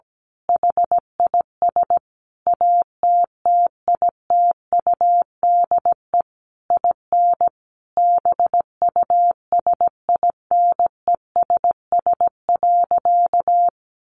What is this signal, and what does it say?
Morse code, heard as a keyed tone, about 17 wpm, 700 Hz. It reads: HIS ATTITUDE IN BUSINESS.
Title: Sentences from Top 1300 Words 17wpm